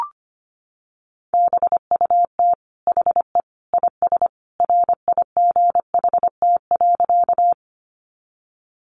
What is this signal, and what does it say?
Morse code, heard as a keyed tone, about 25 wpm, 700 Hz. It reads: BUT HE IS RIGHT.
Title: Sentences from Top 200 Words 25wpm